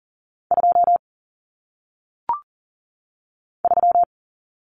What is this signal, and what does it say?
Morse code, heard as a keyed tone, about 40 wpm, 700 Hz. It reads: 2 3
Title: Numbers - 1-Digit 40wpm